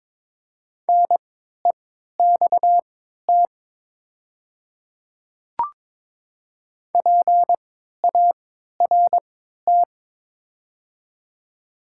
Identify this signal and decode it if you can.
Morse code, heard as a keyed tone, about 22 wpm, spaced as for 12 wpm, 700 Hz. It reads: NEXT PART
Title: 3x Character Spacing - Top 300 Words - 22wpm